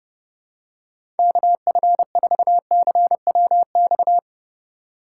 Morse code, heard as a keyed tone, about 29 wpm, 700 Hz. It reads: KF4CWX